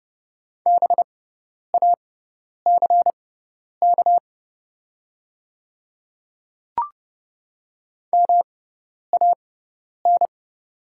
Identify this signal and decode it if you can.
Morse code, heard as a keyed tone, about 30 wpm, spaced as for 10 wpm, 700 Hz. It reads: BACK MAN
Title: 6x Character Spacing - Top 200 Words - 30wpm